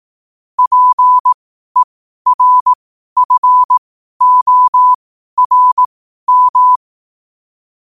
Morse code, heard as a keyed tone, about 18 wpm, spaced as for 12 wpm, 1000 Hz. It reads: PERFORM
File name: qotdfiles / 12wpm